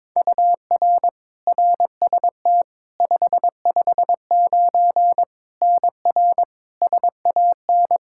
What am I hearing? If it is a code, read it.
Morse code, heard as a keyed tone, about 22 wpm, 700 Hz. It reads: UR RST 559 NR SAN
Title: Practice QSO - Exchange 1 - Beginning - 22wpm